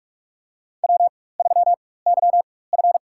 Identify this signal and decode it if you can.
Morse code, heard as a keyed tone, about 44 wpm, spaced as for 21 wpm, 700 Hz. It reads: W3YF
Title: USA Call Signs - 4x Character Spacing - 45wpm